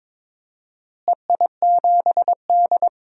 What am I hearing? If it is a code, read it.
Morse code, heard as a keyed tone, about 22 wpm, 700 Hz. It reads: EI7D
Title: DXCC-100 Call Signs - 4 Characters - Repeated 2x - 22wpm